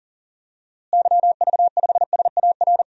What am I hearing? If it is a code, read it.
Morse code, heard as a keyed tone, about 37 wpm, 700 Hz. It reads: YV5SAR
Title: DXCC-100 Call Signs - 6 Characters - Repeated 3x - 40wpm